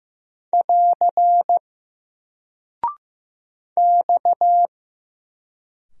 Morse code, heard as a keyed tone, about 15 wpm, 700 Hz. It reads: <AR> X
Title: Single Character (Letter:Number:Punctuation:Prosign) Alt 15wpm